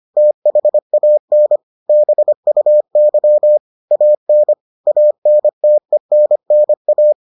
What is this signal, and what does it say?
Morse code, heard as a keyed tone, about 25 wpm, 600 Hz. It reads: THAN BUY AN ANTENNA